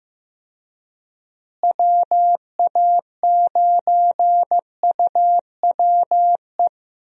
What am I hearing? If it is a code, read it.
Morse code, heard as a keyed tone, about 16 wpm, 700 Hz. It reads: WA9UWE
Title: USA Call Signs - 6 Characters - Repeated 3x - 15wpm